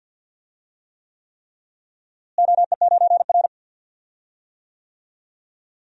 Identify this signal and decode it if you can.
Morse code, heard as a keyed tone, about 46 wpm, 700 Hz. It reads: OE9R
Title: DXCC-100 Call Signs - 4 Characters - Repeated 3x - 50wpm